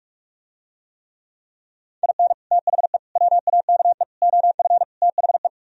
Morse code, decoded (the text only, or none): IN THE WAKE OF THE